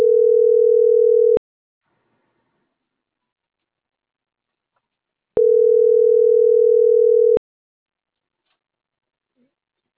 Ringback tone (North America), 440 + 480 Hz.